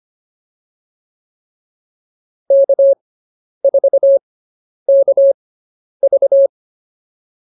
Morse code, heard as a keyed tone, about 25 wpm, spaced as for 10 wpm, 550 Hz. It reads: K4KV